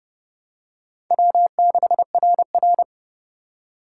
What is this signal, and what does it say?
Morse code, heard as a keyed tone, about 30 wpm, 700 Hz. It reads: W6RR
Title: USA Call Signs - 4 Characters - 3x Character Spacing - 30wpm